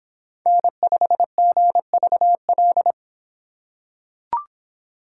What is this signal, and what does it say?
Morse code, heard as a keyed tone, about 23 wpm, 700 Hz. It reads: N5GVL